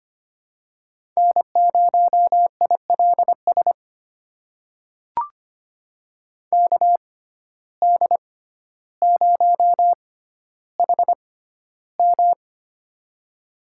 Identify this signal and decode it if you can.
Morse code, heard as a keyed tone, about 25 wpm, spaced as for 13 wpm, 700 Hz. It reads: N0ILS KD0HM
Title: USA Call Signs - 5 Characters - 6x Character Spacing - 25wpm